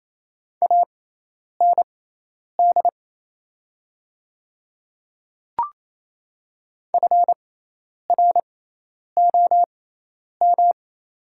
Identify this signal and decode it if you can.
Morse code, heard as a keyed tone, about 28 wpm, spaced as for 10 wpm, 700 Hz. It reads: AND FROM
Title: 6x Character Spacing - Top 300 Words - 28wpm